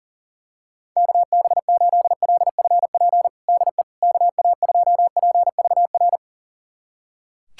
Morse code, heard as a keyed tone, about 40 wpm, 700 Hz. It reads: KB8LFP DE KA2PVR
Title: USA Callsign DE Callsign - 6 Characters - 40wpm